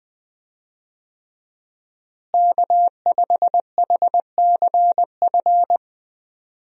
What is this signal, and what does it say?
Morse code, heard as a keyed tone, about 20 wpm, 700 Hz. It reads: K5HCF
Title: USA Call Signs - 5 Characters - Repeated 3x - 20wpm